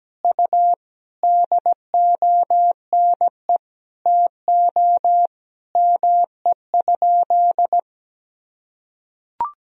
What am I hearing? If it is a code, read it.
Morse code, heard as a keyed tone, about 17 wpm, 700 Hz. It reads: U DONE TO ME?
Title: Sentences from Top 500 Words and Abbreviations 17wpm